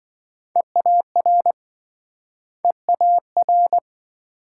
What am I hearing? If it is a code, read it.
Morse code, heard as a keyed tone, about 27 wpm, 700 Hz. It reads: EAR EAR